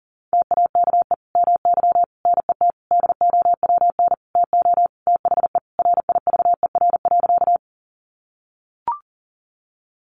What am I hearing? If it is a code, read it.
Morse code, heard as a keyed tone, about 40 wpm, 700 Hz. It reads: TAKE MY NET DOWN TO THE RIVER.
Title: Sentences from Top 1400 Words 40wpm